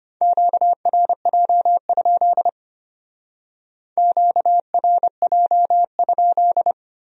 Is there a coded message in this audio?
Yes: Morse code from 0.2 to 6.7 s, heard as a keyed tone, about 33 wpm, 700 Hz. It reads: QRJ? QRJ?